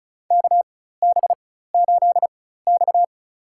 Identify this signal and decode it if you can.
Morse code, heard as a keyed tone, about 36 wpm, spaced as for 16 wpm, 700 Hz. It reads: KB8X
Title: USA Call Signs - 4 Characters - 4x Character Spacing - 35wpm